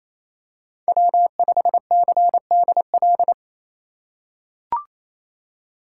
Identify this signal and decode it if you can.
Morse code, heard as a keyed tone, about 28 wpm, 700 Hz. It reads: W5CDL
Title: USA Call Signs - 5 Characters - 2x Character Spacing - 28wpm